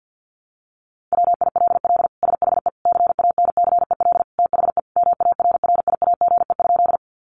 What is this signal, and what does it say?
Morse code, heard as a keyed tone, about 48 wpm, 700 Hz. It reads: WILL SHE CANCEL THE MARRIAGE?
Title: Sentences from Top 1200 Words 50wpm